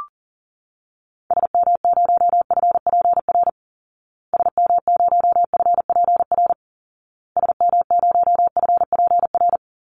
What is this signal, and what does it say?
Morse code, heard as a keyed tone, about 40 wpm, 700 Hz. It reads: SM0FPR SM0FPR SM0FPR
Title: Call Signs Repeated 2x - 40wpm